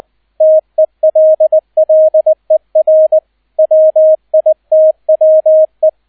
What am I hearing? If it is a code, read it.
Morse code, heard as a keyed tone, about 20 wpm, 630 Hz, with radio noise. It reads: TELLER WITWE